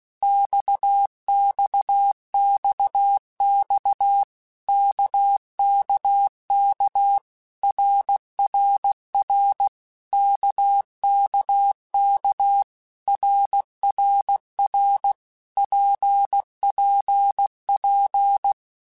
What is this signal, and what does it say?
Morse code, heard as a keyed tone, about 16 wpm, 800 Hz. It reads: XXXX KKK RRR KKK RRR PPP
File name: Basic Letter Groups By7Groups / RKPX / RKPX@15